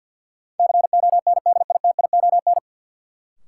Morse code, heard as a keyed tone, about 53 wpm, 700 Hz. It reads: CONDITION
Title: Sentences from Top 1000 Words - New Word Drill - 50wpm